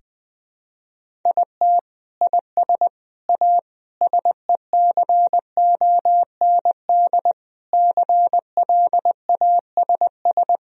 Morse code, heard as a keyed tone, about 20 wpm, 700 Hz. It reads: IT IS A SECOND CLASS
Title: Sentences from Top 1000 Words - Review Part 2 - 20wpm